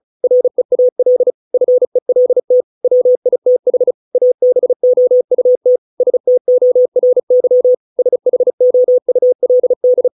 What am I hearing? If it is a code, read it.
Morse code, heard as a keyed tone, about 34 wpm, 500 Hz. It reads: REAL FELT WITH ABOUT STORY SHOULD
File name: 632 words / 632 Words 35WPM